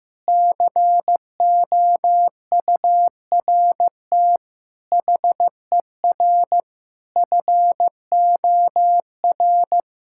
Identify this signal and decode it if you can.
Morse code, heard as a keyed tone, about 15 wpm, 700 Hz. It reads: COURT HER FOR
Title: Sentences from Top 1600 Words 15wpm